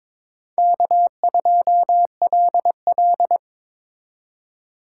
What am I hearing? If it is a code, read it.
Morse code, heard as a keyed tone, about 26 wpm, 700 Hz. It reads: K2LL
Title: Speed Racing - USA 1x2 Call Signs 22wpm